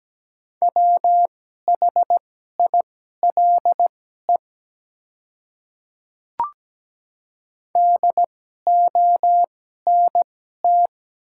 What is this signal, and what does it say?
Morse code, heard as a keyed tone, about 17 wpm, spaced as for 12 wpm, 700 Hz. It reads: WHILE DONT
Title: 2x Character Spacing - Top 300 Words - 17wpm